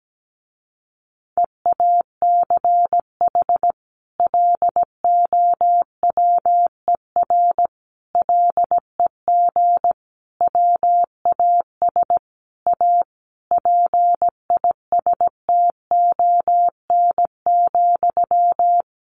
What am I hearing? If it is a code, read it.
Morse code, heard as a keyed tone, about 17 wpm, 700 Hz. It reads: EACH LOWER LEG WAS A PISTON,
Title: A Sound Of Thunder 17wpm